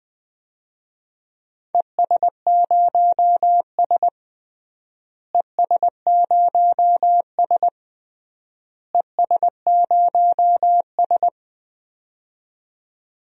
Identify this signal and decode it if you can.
Morse code, heard as a keyed tone, about 19 wpm, 700 Hz. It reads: ES0S ES0S ES0S
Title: DXCC-100 Call Signs - 4 Characters - Repeated 3x - 20wpm